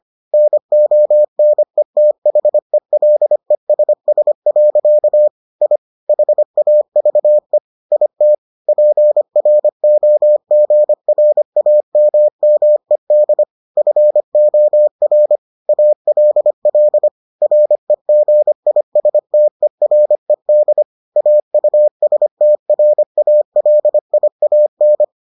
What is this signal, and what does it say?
Morse code, heard as a keyed tone, about 25 wpm, 600 Hz. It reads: NONETHELESS. I HAVE IT PROGRAMMED FOR ALL REGISTERED AUSTRALIAN